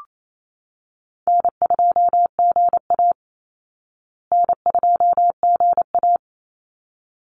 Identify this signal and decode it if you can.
Morse code, heard as a keyed tone, about 28 wpm, 700 Hz. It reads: N2GA N2GA
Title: Easy Call Signs Repeated 2x - 28wpm